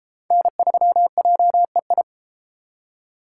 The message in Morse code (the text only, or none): N3JEI